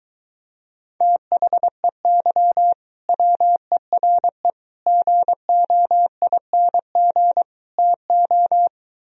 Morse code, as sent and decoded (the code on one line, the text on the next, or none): - .... -.-- / .-- . .-. / --. --- .. -. --. / - ---
THEY WERE GOING TO